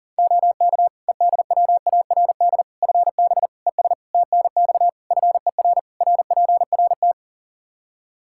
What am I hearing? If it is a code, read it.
Morse code, heard as a keyed tone, about 40 wpm, 700 Hz. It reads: OK EDWARD FB ES TNX FER RPRT
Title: Practice QSO - Exchange 2 - Info - 40wpm